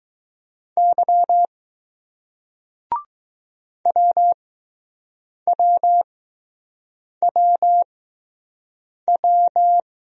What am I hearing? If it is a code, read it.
Morse code, heard as a keyed tone, about 20 wpm, spaced as for 6 wpm, 700 Hz. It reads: Y WWWW